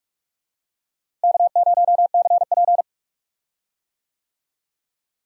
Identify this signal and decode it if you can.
Morse code, heard as a keyed tone, about 45 wpm, 700 Hz. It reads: K0CP
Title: USA Call Signs - 4 Characters - Repeated 2x - 45wpm